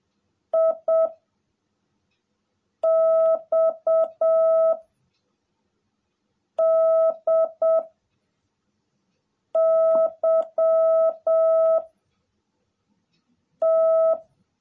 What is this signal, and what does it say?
Morse code, heard as a keyed tone, about 7 wpm, 650 Hz. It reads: I X D Y T
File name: jh8cbh / 2021cw / cw18a